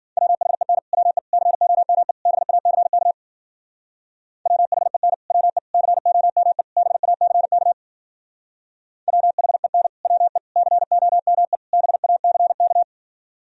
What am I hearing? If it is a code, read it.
Morse code, heard as a keyed tone, about 50 wpm, 700 Hz. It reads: WHEN WE COME BACK WHEN WE COME BACK WHEN WE COME BACK
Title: Speed-Racing - Sets of 4 Words 40wpm